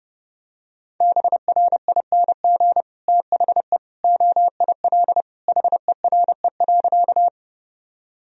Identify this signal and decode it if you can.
Morse code, heard as a keyed tone, about 32 wpm, 700 Hz. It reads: BRING THE OIL HERE.
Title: Sentences from Top 1000 Words - Review Part 4 - 30wpm